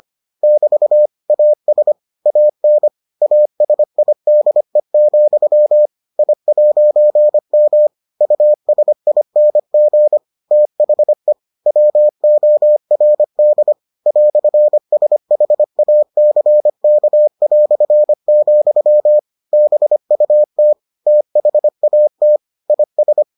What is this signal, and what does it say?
Morse code, heard as a keyed tone, about 25 wpm, 600 Hz. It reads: <BT> AS AN ASIDE, I'M USING THE WORD "SHACK", BUT THAT IS